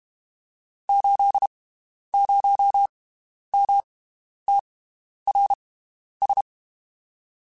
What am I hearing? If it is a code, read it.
Morse code, heard as a keyed tone, about 32 wpm, spaced as for 11 wpm, 800 Hz. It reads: 80MTRS
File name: (spaced) / QSOs Spaced-mp3s / WD0FBT 1 @30 sp